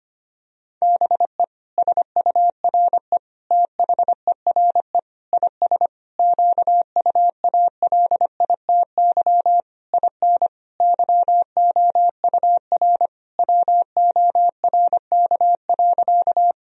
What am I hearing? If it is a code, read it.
Morse code, heard as a keyed tone, about 25 wpm, 700 Hz. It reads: BE SURE THERE IS QUALITY IN YOUR WORK.
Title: Sentences from Top 1000 Words - Review Part 4 - 25wpm